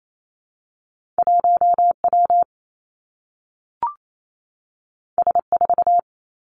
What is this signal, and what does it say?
Morse code, heard as a keyed tone, about 28 wpm, 700 Hz. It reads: 1 W S 4
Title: Two Challenging Letter-Number Combinations 28wpm